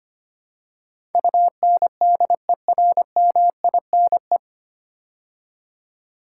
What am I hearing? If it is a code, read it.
Morse code, heard as a keyed tone, about 25 wpm, 690 Hz. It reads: UNDERMINE
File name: practice / 25wpm / 9 Letter Words - First 3 Letters Given - 25wpm